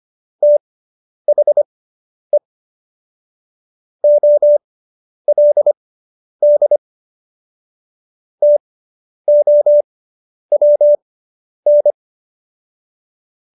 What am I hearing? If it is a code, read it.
Morse code, heard as a keyed tone, about 25 wpm, spaced as for 10 wpm, 590 Hz. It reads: THE OLD TOWN